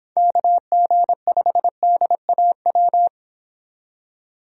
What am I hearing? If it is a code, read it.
Morse code, heard as a keyed tone, about 26 wpm, 700 Hz. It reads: KG5DAW
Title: Speed Racing - USA 2x3 Call Signs 22wpm